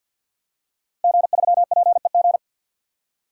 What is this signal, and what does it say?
Morse code, heard as a keyed tone, about 50 wpm, 700 Hz. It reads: G3PEG
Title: Code Groups - 5 Letter:Numbers - 50wpm